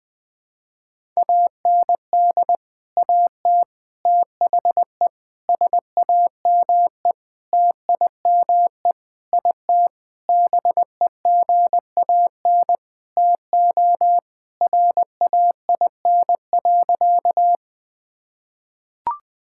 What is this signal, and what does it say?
Morse code, heard as a keyed tone, about 21 wpm, 700 Hz. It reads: AND AT THE SAME TIME IT BEGAN TO RAIN.
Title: Sentences with 5-Word Phrases - Spoken Beginning - 20wpm